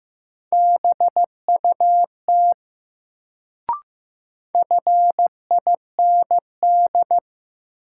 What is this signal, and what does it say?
Morse code, heard as a keyed tone, about 15 wpm, 700 Hz. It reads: BUT FIND
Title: Sets of 2 Words - New Word Drill - 15wpm